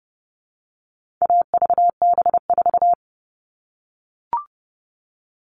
Morse code, heard as a keyed tone, about 30 wpm, 700 Hz. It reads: AVB4
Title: All Cut Numbers - 4-Digit 30wpm